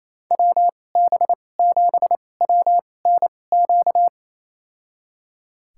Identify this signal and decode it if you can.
Morse code, heard as a keyed tone, about 28 wpm, spaced as for 20 wpm, 700 Hz. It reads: WB7WNQ
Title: USA Call Signs - 6 Characters - 2x Character Spacing - 28wpm